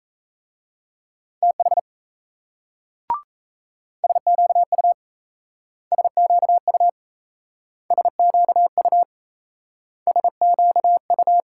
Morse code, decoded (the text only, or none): TH SQU SQU SQU SQU